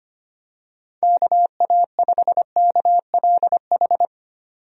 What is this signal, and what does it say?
Morse code, heard as a keyed tone, about 25 wpm, 700 Hz. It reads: KA5KLH
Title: USA Call Signs - 6 Characters - 25wpm